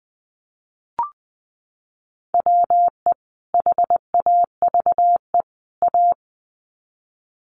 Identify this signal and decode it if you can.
Morse code, heard as a keyed tone, about 20 wpm, 700 Hz. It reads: WE HAVE A
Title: Sets of 3 Words 20wpm